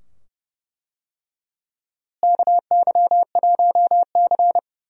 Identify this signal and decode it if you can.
Morse code, heard as a keyed tone, about 31 wpm, 700 Hz. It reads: KY1C
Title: USA Call Signs - 4 Characters - Repeated 2x - 30wpm